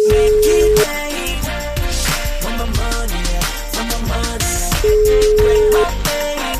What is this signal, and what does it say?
Ringback tone (Europe, 425 Hz), with noise.